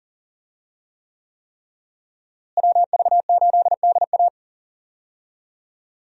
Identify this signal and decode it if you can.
Morse code, heard as a keyed tone, about 40 wpm, 700 Hz. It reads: WV8DA
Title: USA Call Signs - 5 Characters - Repeated 3x - 40wpm